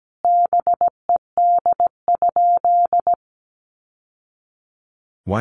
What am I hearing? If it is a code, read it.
Morse code, heard as a keyed tone, about 17 wpm, 700 Hz. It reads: BED?
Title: Sentences from Top 1200 Words 17wpm